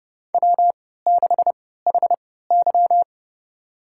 Morse code, heard as a keyed tone, about 30 wpm, spaced as for 17 wpm, 700 Hz. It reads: W6HY
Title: USA Call Signs - 4 Characters - 3x Character Spacing - 30wpm